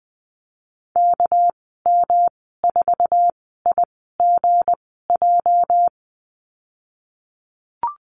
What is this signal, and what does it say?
Morse code, heard as a keyed tone, about 20 wpm, spaced as for 15 wpm, 700 Hz. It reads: KM4IGJ
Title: 219 Lesson 35g - Number 0 - Call Signs 20wpm